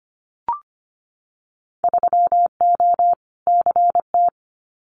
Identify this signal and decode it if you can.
Morse code, heard as a keyed tone, about 25 wpm, 690 Hz. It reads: 3O CT
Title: Winter Field Day Exchanges 25wpm